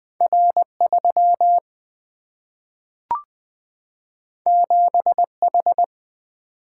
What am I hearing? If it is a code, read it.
Morse code, heard as a keyed tone, about 20 wpm, 700 Hz. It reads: R3 7H
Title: Single-Letter Single-Number 20wpm